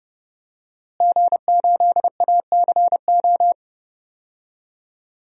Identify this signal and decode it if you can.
Morse code, heard as a keyed tone, about 30 wpm, 700 Hz. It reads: G8ACO
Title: DXCC-100 Call Signs - 5 Characters - Repeated 3x - 30wpm